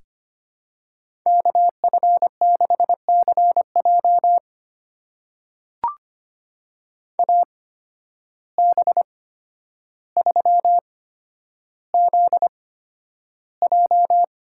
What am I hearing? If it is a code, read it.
Morse code, heard as a keyed tone, about 25 wpm, spaced as for 11 wpm, 700 Hz. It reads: KF6CJ AB3ZJ
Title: USA Call Signs - 5 Characters - 8x Character Spacing - 25wpm